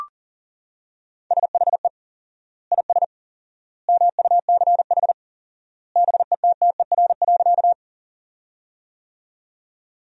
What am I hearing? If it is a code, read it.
Morse code, heard as a keyed tone, about 40 wpm, 700 Hz. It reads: SHE IS MUCH BETTER.
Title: 4x Word-Spacing - Sentences from Top 400 Words - 40wpm